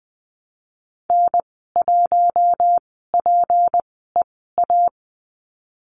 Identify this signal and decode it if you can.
Morse code, heard as a keyed tone, about 19 wpm, spaced as for 14 wpm, 700 Hz. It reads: N1PEA
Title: 207 Lesson 33g - Letter j - Call Signs 20wpm